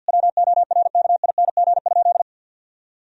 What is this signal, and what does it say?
Morse code, heard as a keyed tone, about 50 wpm, 700 Hz. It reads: WORKING?
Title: Sentences from Top 1000 Words - Review Part 4 - 50wpm